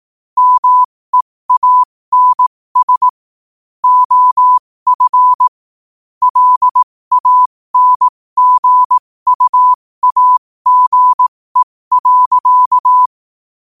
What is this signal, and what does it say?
Morse code, heard as a keyed tone, about 18 wpm, spaced as for 15 wpm, 1000 Hz. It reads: MEANS OF LANGUAGE.
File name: qotdfiles / 15wpm